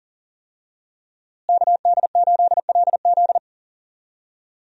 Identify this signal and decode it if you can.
Morse code, heard as a keyed tone, about 40 wpm, 700 Hz. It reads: KD8LZ
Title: USA Call Signs - 5 Characters - Repeated 3x - 40wpm